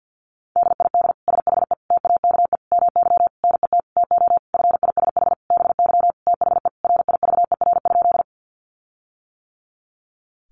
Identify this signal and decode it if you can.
Morse code, heard as a keyed tone, about 50 wpm, 700 Hz. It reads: DID SHE TAKE MY NET TO FISH BY THE RIVER?
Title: Sentences from Top 1400 Words 50wpm